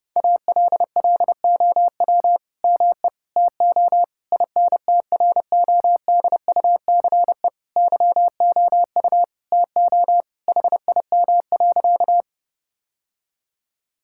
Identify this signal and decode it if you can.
Morse code, heard as a keyed tone, about 30 wpm, 700 Hz. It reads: ALLOW ME TO INTRODUCE YOU TO HIM.
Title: Sentences from Top 1600 Words 30wpm